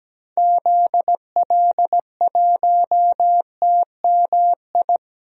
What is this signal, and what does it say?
Morse code, heard as a keyed tone, about 17 wpm, 700 Hz. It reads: ZL1TMI
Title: DXCC-100 Call Signs - 6 Characters - 17wpm